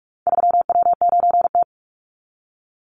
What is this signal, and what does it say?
Morse code, heard as a keyed tone, about 44 wpm, 700 Hz. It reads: 3W9T
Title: Call Signs - 45wpm